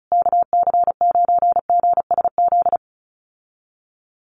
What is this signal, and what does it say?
Morse code, heard as a keyed tone, about 34 wpm, 700 Hz. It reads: KC9GSZ
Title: USA Call Signs Repeated 3x 35wpm